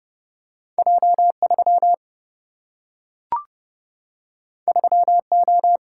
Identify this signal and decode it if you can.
Morse code, heard as a keyed tone, about 30 wpm, 700 Hz. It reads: J3 3O
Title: Single-Letter Single-Number 30wpm